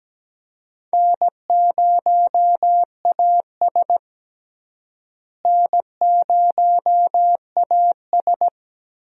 Morse code, heard as a keyed tone, about 17 wpm, 700 Hz. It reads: N0AS N0AS
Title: USA Call Signs - 4 Characters - Repeated 2x - 17wpm